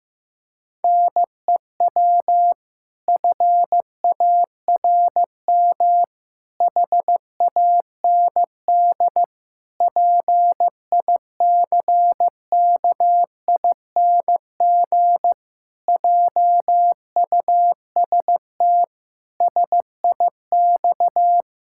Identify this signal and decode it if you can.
Morse code, heard as a keyed tone, about 16 wpm, 700 Hz. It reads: NEW FARM HAND PICKING JUST SIX
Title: Pangram Challenge v2 - 15wpm